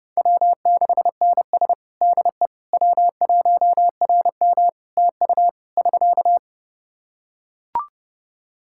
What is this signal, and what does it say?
Morse code, heard as a keyed tone, about 30 wpm, 700 Hz. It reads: W6NS DE W1RM TU <SK>
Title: K1USN SST QSO Practice 30wpm